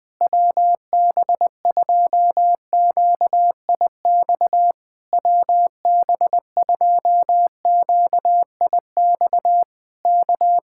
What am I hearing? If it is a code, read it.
Morse code, heard as a keyed tone, about 20 wpm, 700 Hz. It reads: WB2QIX WB2QIX K
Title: Speed-Racing Common QSO Phrases Encore - 20wpm